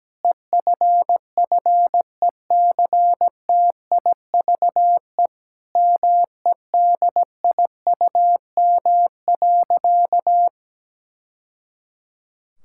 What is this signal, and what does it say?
Morse code, heard as a keyed tone, about 17 wpm, 700 Hz. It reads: EFFECTIVE MEDIUM.
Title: Sentences from Top 1000 Words - Review Part 6 - 17wpm